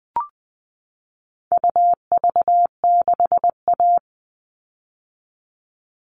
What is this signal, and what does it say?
Morse code, heard as a keyed tone, about 20 wpm, 700 Hz. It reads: UV6A
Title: All Cut Numbers - 4-Digit 20wpm